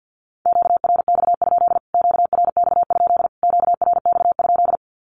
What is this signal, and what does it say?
Morse code, heard as a keyed tone, about 52 wpm, 700 Hz. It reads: QRX? QRX? QRX?
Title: Q-Codes Repeated 3x 50wpm